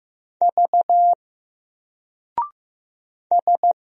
Morse code, heard as a keyed tone, about 15 wpm, 700 Hz. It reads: V S
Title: Single Letter-Number Alt 15wpm